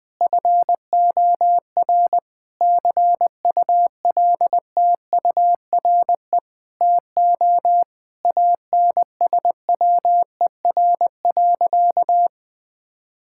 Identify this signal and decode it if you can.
Morse code, heard as a keyed tone, about 20 wpm, 700 Hz. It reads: FOR CULTURE TO ANSWER.
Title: Sentences from Top 1000 Words - Review Part 6 - 20wpm